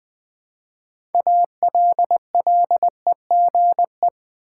Morse code, heard as a keyed tone, about 20 wpm, 700 Hz. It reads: ALLEGE
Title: Build A Word - 2 Letters At A Time - 6-Letter Words - 20wpm